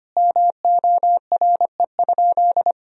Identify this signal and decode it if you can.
Morse code, heard as a keyed tone, about 25 wpm, 700 Hz. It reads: MORE?
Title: Sentences from Top 1000 Words - Review Part 1 - 25wpm